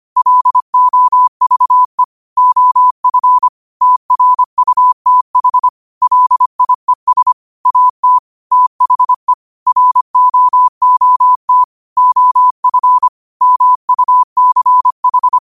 Morse code, heard as a keyed tone, about 25 wpm, 1000 Hz. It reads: LOVE OF TRUTH LIES AT THE ROOT OF MUCH